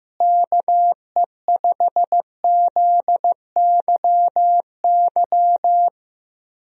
Morse code, heard as a keyed tone, about 16 wpm, 700 Hz. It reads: KE5ZYY
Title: USA Call Signs - 6 Characters - Repeated 2x - 15wpm